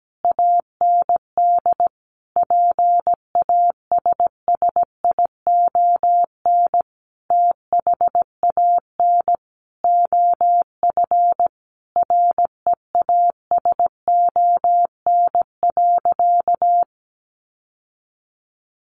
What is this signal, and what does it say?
Morse code, heard as a keyed tone, about 17 wpm, 700 Hz. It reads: AND PASSION THAN OF REASON.
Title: Sentences from Top 1300 Words 17wpm